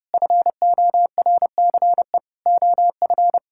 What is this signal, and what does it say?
Morse code, heard as a keyed tone, about 32 wpm, 700 Hz. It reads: FORCE OF
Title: Sentences from Top 1000 Words - Review Part 5 - 30wpm